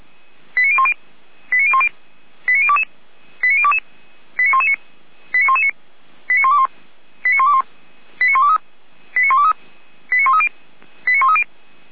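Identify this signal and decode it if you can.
Five-tone selective call (ZVEI), heard as quick runs of five tones, tones of about 70 ms. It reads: 89010 89010 89022 89022 89119 89119 89121 89121 89123 89123 89130 89130
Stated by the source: ZVEI five-tone selective call 89011 received on an SDR.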